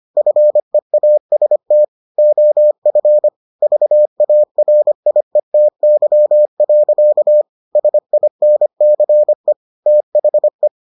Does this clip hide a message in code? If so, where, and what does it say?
Yes: Morse code from 0.2 to 10.7 s, heard as a keyed tone, about 25 wpm, 600 Hz. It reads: FEAST OF VARIETY. SINCE THE